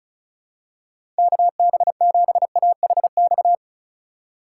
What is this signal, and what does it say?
Morse code, heard as a keyed tone, about 34 wpm, 700 Hz. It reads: KB7AHX
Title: USA Call Signs - 6 Characters - 2x Character Spacing - 35wpm